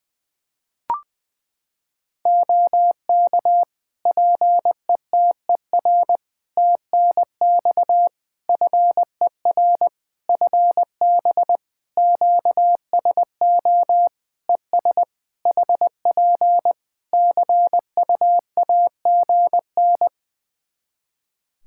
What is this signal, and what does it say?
Morse code, heard as a keyed tone, about 20 wpm, 700 Hz. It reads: OK PETER TNX FER FB QSO ES HP CUAGN
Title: K1USN SST QSO Practice 20wpm